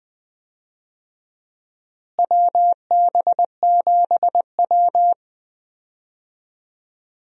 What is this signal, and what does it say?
Morse code, heard as a keyed tone, about 20 wpm, 700 Hz. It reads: WB7W
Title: USA Call Signs - 4 Characters - Repeated 3x - 20wpm